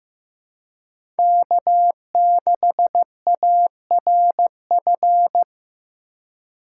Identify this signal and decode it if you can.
Morse code, heard as a keyed tone, about 16 wpm, 700 Hz. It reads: K6ARF
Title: USA Call Signs - 5 Characters - 2x Character Spacing - 15wpm